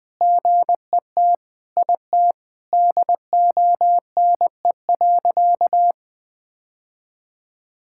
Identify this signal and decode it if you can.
Morse code, heard as a keyed tone, about 20 wpm, 700 Hz. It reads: GET IT DONE.
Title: Sentences from Top 1000 Words - Review Part 5 - 20wpm